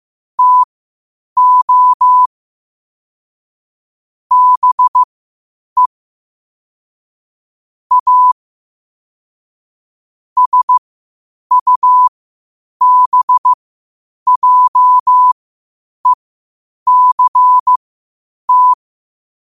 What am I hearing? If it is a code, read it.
Morse code, heard as a keyed tone, about 15 wpm, spaced as for 8 wpm, 1000 Hz. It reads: TO BE A SUBJECT
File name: qotdfiles / 8wpm